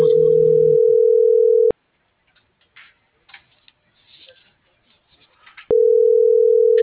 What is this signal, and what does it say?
Ringback tone (North America), 440 + 480 Hz.